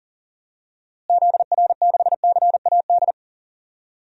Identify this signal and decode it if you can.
Morse code, heard as a keyed tone, about 40 wpm, 700 Hz. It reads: ZR6CAD
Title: DXCC-100 Call Signs - 6 Characters - Repeated 2x - 40wpm